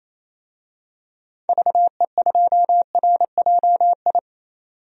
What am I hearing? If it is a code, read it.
Morse code, heard as a keyed tone, about 27 wpm, 700 Hz. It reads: VE2RJI
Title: DXCC-100 Call Signs - 6 Characters - 28wpm